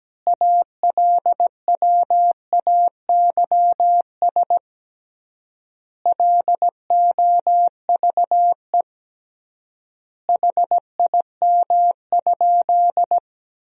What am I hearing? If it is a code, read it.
Morse code, heard as a keyed tone, about 17 wpm, 700 Hz. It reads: ALWAYS LOVE HIM?
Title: 3x Word-Spacing - Sentences from Top 400 Words - 17wpm